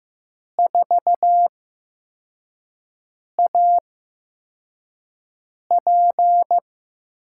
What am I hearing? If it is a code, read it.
Morse code, heard as a keyed tone, about 15 wpm, spaced as for 4 wpm, 700 Hz. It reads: 4AP
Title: USA Call Signs - 5 Characters - 8x Character Spacing - 15wpm